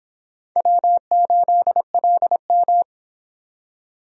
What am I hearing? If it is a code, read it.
Morse code, heard as a keyed tone, about 23 wpm, 700 Hz. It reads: W8LM